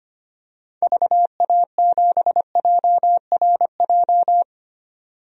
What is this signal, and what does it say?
Morse code, heard as a keyed tone, about 25 wpm, 690 Hz. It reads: VA7JRJ
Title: DXCC-100 Call Signs - 6 Characters - 25wpm